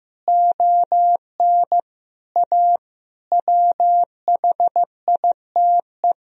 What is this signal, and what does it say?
Morse code, heard as a keyed tone, about 15 wpm, 700 Hz. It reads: ON A WHITE
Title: Sentences from Top 1000 Words - Review Part 1 - 15wpm